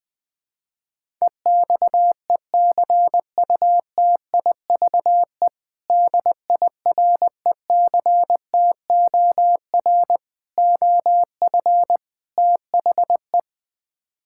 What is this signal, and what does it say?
Morse code, heard as a keyed tone, about 21 wpm, 700 Hz. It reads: EXECUTIVE DIRECTOR OF THE
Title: Build A Phrase - 4-Word Phrases - 20wpm